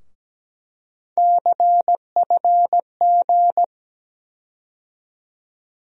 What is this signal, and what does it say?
Morse code, heard as a keyed tone, about 17 wpm, 700 Hz. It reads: CFG
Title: USA Call Sign Postfixes - 3 Letters 17wpm